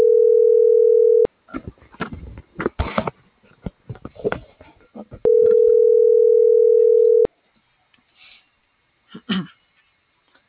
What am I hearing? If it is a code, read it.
Ringback tone (North America), 440 + 480 Hz.